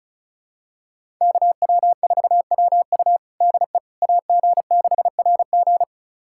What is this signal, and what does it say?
Morse code, heard as a keyed tone, about 36 wpm, 700 Hz. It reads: KW4WU DE AG6RG
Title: USA Callsign DE Callsign - 5 Characters - 35wpm